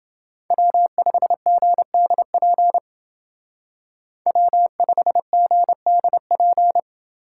Morse code, heard as a keyed tone, about 28 wpm, 700 Hz. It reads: W5GDP W5GDP